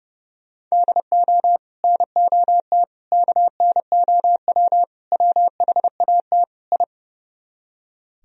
Morse code, heard as a keyed tone, about 40 wpm, 700 Hz. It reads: DO NOT KNOW WHAT I